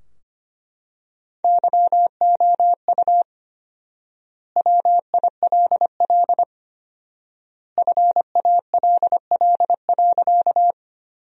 Morse code, heard as a keyed tone, about 25 wpm, 700 Hz. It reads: YOU WILL FALL.
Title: 4x Word-Spacing - Sentences from Top 500 Words - 25wpm